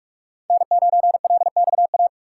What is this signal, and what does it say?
Morse code, heard as a keyed tone, about 42 wpm, 700 Hz. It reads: N9LXA
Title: USA Call Signs - 5 Characters - 4x Character Spacing - 45wpm